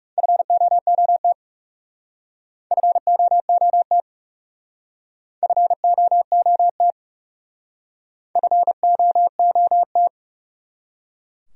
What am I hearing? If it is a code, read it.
Morse code, heard as a keyed tone, about 40 wpm, 700 Hz. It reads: FOOT FOOT FOOT FOOT